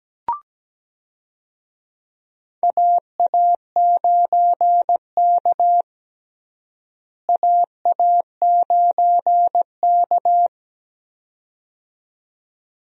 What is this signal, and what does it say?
Morse code, heard as a keyed tone, about 17 wpm, 700 Hz. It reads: AA9K AA9K
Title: USA Call Signs - 4 Characters - Repeated 2x - 17wpm